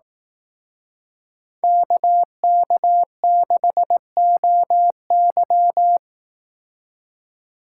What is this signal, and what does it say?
Morse code, heard as a keyed tone, about 18 wpm, 700 Hz. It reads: KK6OY
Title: Speed Racing - USA 2x2 Call Signs 15wpm